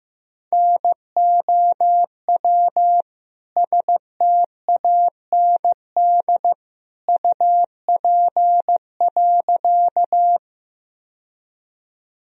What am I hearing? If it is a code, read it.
Morse code, heard as a keyed tone, about 15 wpm, 700 Hz. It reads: NOW STAND UP.
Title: Sentences from Top 1000 Words - Review Part 1 - 15wpm